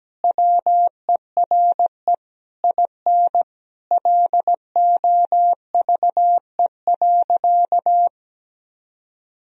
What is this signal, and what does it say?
Morse code, heard as a keyed tone, about 17 wpm, 700 Hz. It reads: WERE IN LOVE.
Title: Sentences from Top 1000 Words - Review Part 4 - 17wpm